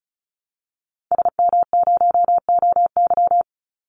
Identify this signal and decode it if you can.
Morse code, heard as a keyed tone, about 34 wpm, 700 Hz. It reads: SM0OY
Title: Call Signs Repeated 2x - 35wpm